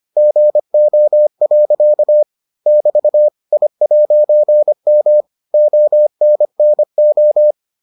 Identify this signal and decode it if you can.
Morse code, heard as a keyed tone, about 25 wpm, 600 Hz. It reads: GO. <BT> I'M ONNO